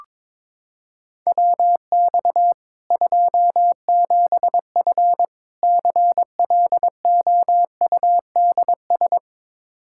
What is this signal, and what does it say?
Morse code, heard as a keyed tone, about 22 wpm, 700 Hz. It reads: WX 27F CLOUDS
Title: K1USN SST QSO Practice 22wpm